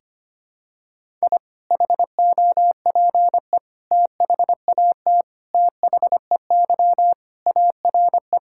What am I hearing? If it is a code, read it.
Morse code, heard as a keyed tone, about 25 wpm, 700 Hz. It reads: I HOPE THAT THEY ARE
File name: practice / 25wpm / Sentences with 3-Word Phrases - 25wpm